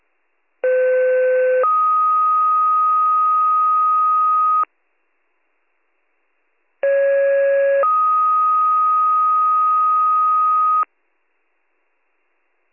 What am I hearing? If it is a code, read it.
Two-tone page, heard as a one-second tone, then a three-second tone, about 510 then 1250 Hz, about 550 then 1250 Hz. It is heard from 0.6 to 10.8 s.